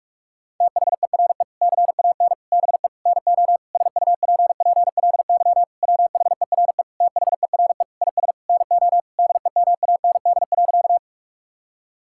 Morse code, heard as a keyed tone, about 42 wpm, 700 Hz. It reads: THERE CAN BE NO SUPPLY WHERE THERE IS NO DEMAND.
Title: Sentences from Top 1000 Words - Review Part 5 - 45wpm